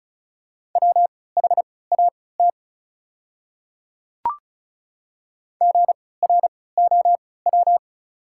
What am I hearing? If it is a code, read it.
Morse code, heard as a keyed tone, about 36 wpm, spaced as for 20 wpm, 690 Hz. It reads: WHAT GROW
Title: 3x Character Spacing - Top 500 Words - 35wpm